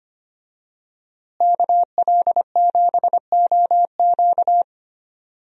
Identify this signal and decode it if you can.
Morse code, heard as a keyed tone, about 25 wpm, 700 Hz. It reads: KL7OQ
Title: DXCC-100 Call Signs - 5 Characters - Repeated 3x - 25wpm